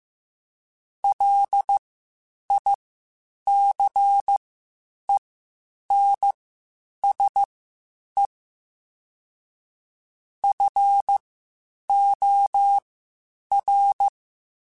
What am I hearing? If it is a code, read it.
Morse code, heard as a keyed tone, about 15 wpm, spaced as for 8 wpm, 800 Hz. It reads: LICENSE FOR
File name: QSOs Spaced-mp3s/kc7haa @13 sp